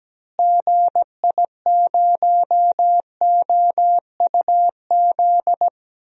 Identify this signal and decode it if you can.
Morse code, heard as a keyed tone, about 17 wpm, 700 Hz. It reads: GI0OUZ
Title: DXCC-100 Call Signs - 6 Characters - Repeated 3x - 17wpm